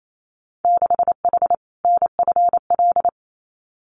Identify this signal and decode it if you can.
Morse code, heard as a keyed tone, about 28 wpm, 690 Hz. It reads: 6H NFL
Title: Winter Field Day Exchanges Encore 28wpm